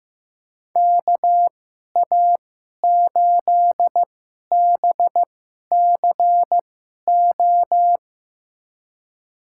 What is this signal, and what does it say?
Morse code, heard as a keyed tone, about 15 wpm, spaced as for 11 wpm, 700 Hz. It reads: KA8BCO
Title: USA Call Signs - 2x Character Spacing - 15wpm